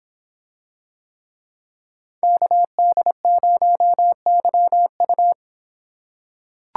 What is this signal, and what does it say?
Morse code, heard as a keyed tone, about 26 wpm, 700 Hz. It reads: KD0YU